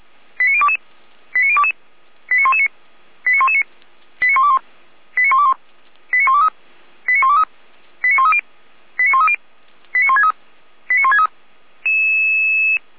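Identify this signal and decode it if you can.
Five-tone selective call (ZVEI), heard as quick runs of five tones, tones of about 80 ms. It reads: 89022 89022 89119 89119 89121 89121 89123 89123 89130 89130 89163 89163
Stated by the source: Five-tone ZVEI selective call 89017, received on SDR.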